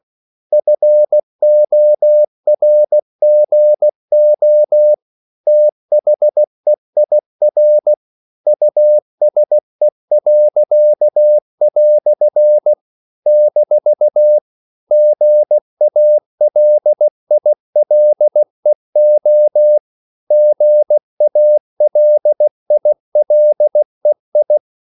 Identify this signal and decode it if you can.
Morse code, heard as a keyed tone, about 16 wpm, 600 Hz. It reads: FORGO THEIR USE." - GALILEO GALILEI
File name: CW / Practice / EnglishQuotations0116